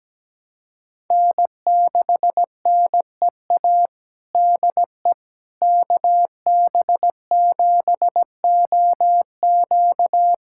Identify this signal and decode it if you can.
Morse code, heard as a keyed tone, about 17 wpm, 700 Hz. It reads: N6NEA DE KB7OQ
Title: USA Callsign DE Callsign - 5 Characters - 17wpm